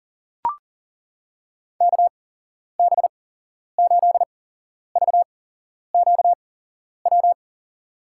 Morse code, heard as a keyed tone, about 37 wpm, spaced as for 11 wpm, 700 Hz. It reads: KB8VQW
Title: USA Call Signs - 8x Character Spacing - 40wpm